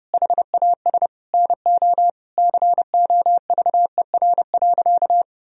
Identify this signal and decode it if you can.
Morse code, heard as a keyed tone, about 32 wpm, 700 Hz. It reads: HAS NO COVER.
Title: Sentences from Top 1000 Words - Review Part 1 - 30wpm